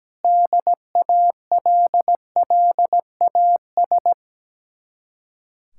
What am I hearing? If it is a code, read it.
Morse code, heard as a keyed tone, about 17 wpm, 700 Hz. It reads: DALLAS
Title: Speed-Racing Common QSO Phrases - 17wpm